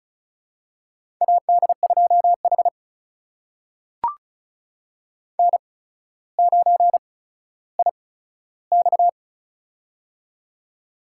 Morse code, heard as a keyed tone, about 34 wpm, spaced as for 15 wpm, 700 Hz. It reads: AD2H N9IX
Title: USA Call Signs - 4 Characters - 8x Character Spacing - 35wpm